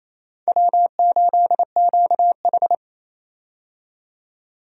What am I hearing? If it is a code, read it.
Morse code, heard as a keyed tone, about 28 wpm, 690 Hz. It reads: W8QH
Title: USA Call Signs - 4 Characters - 28wpm